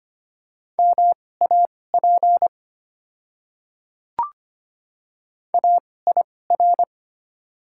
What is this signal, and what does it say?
Morse code, heard as a keyed tone, about 25 wpm, spaced as for 18 wpm, 700 Hz. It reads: MAP AIR
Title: 2x Character Spacing - Top 500 Words - 25wpm